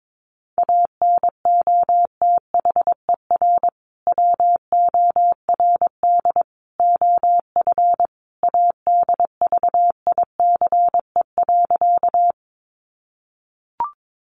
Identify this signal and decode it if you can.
Morse code, heard as a keyed tone, about 22 wpm, 700 Hz. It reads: ANOTHER WORD OF ADVICE.
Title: Sentences from Top 1100 Words 22wpm